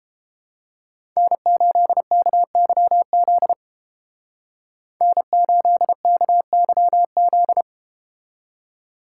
Morse code, heard as a keyed tone, about 31 wpm, 700 Hz. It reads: N8KYZ N8KYZ